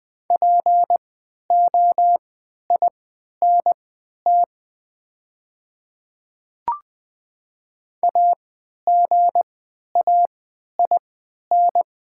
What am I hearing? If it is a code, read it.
Morse code, heard as a keyed tone, about 20 wpm, spaced as for 11 wpm, 700 Hz. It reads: POINT AGAIN
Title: 3x Character Spacing - Top 200 Words - 20wpm